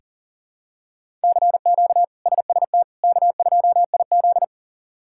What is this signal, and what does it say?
Morse code, heard as a keyed tone, about 40 wpm, 700 Hz. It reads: CQ SST K2IZ
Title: K1USN SST QSO Practice - 40wpm